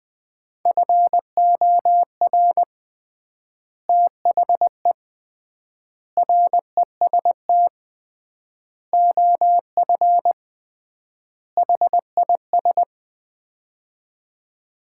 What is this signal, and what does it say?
Morse code, heard as a keyed tone, about 20 wpm, 690 Hz. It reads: FOR THE REST OF HIS
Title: 3x Word-Spacing - Sets of 5 Words - 20wpm